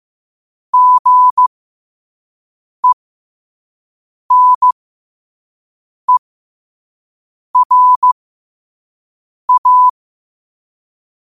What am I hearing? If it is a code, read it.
Morse code, heard as a keyed tone, about 15 wpm, spaced as for 5 wpm, 1000 Hz. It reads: GENERA
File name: qotdfiles / 5wpm